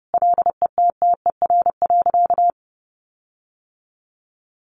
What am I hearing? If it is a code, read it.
Morse code, heard as a keyed tone, about 30 wpm, 700 Hz. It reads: LETTER.
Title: Sentences from Top 1400 Words 30wpm